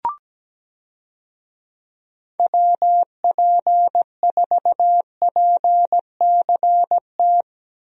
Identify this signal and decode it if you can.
Morse code, heard as a keyed tone, about 17 wpm, 700 Hz. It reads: WP4PCT
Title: USA Call Signs - 6 Characters - Repeated 3x - 17wpm